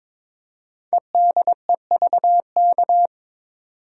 Morse code, heard as a keyed tone, about 22 wpm, 700 Hz. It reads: EDEVK
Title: Code Groups - 5 Letters - 22wpm